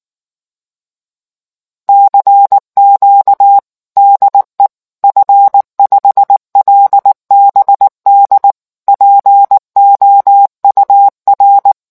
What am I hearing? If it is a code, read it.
Morse code, heard as a keyed tone, about 19 wpm, 800 Hz. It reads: CQ DE F5LBD POUR